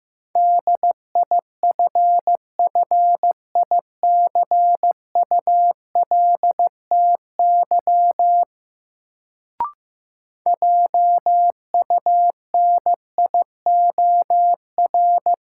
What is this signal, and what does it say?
Morse code, heard as a keyed tone, about 15 wpm, 700 Hz. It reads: DIFFICULTY JUNIOR
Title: Sentences from Top 1500 Words - New Word Drill Alt - 15wpm